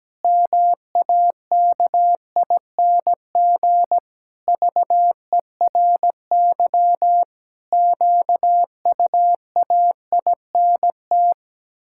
Morse code, heard as a keyed tone, about 17 wpm, 700 Hz. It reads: MAKING VERY QUAINT
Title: Pangram Challenge v2 - 17wpm